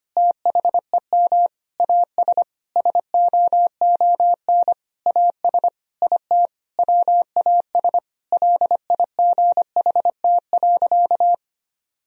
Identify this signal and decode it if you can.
Morse code, heard as a keyed tone, about 25 wpm, 700 Hz. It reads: THEM AS SOON AS IT WAS LIGHT.
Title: Sentences from Top 1000 Words - Review Part 1 - 25wpm